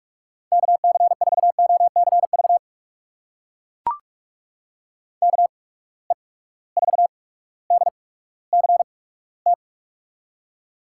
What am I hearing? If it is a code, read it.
Morse code, heard as a keyed tone, about 44 wpm, spaced as for 19 wpm, 700 Hz. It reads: KC4OCV KE4DCT
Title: USA Call Signs - 6 Characters - 8x Character Spacing - 45wpm